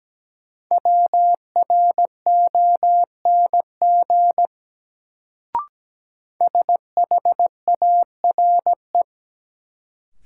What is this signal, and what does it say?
Morse code, heard as a keyed tone, about 17 wpm, 700 Hz. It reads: WRONG SHARE
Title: Sentences from Top 900 Words - New Word Drill - 17wpm